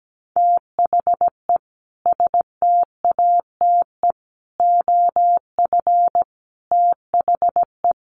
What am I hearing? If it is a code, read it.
Morse code, heard as a keyed tone, about 17 wpm, 700 Hz. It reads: THE STATE OF THE
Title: Sentences from Top 1400 Words 17wpm